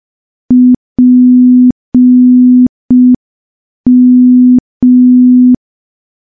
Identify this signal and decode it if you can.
Morse code, heard as a keyed tone, about 5 wpm, 250 Hz. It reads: PM
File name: morse / morse-training-05WPM